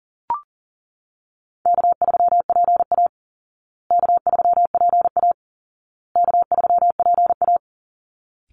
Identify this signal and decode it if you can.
Morse code, heard as a keyed tone, about 40 wpm, 700 Hz. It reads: K3PA K3PA K3PA
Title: Easy Call Signs Repeated 3x - 40wpm